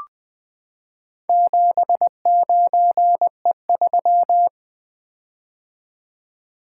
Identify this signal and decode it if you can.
Morse code, heard as a keyed tone, about 20 wpm, 700 Hz. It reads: 79E3
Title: Four Letter-Number Combinations - 20wpm